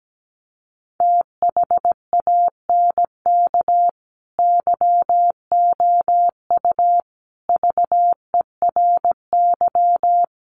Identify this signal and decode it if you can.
Morse code, heard as a keyed tone, about 17 wpm, 700 Hz. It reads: THANK YOU VERY
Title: Sets of 4 Words 17wpm